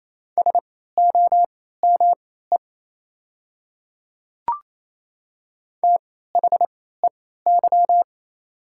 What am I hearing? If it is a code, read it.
Morse code, heard as a keyed tone, about 28 wpm, spaced as for 16 wpm, 690 Hz. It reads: SOME THEY
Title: 3x Character Spacing - Top 100 Words - 28wpm